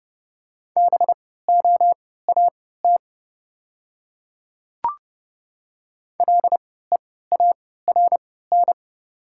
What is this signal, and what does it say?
Morse code, heard as a keyed tone, about 30 wpm, spaced as for 17 wpm, 690 Hz. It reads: BOAT LEARN
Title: 3x Character Spacing - Top 500 Words - 30wpm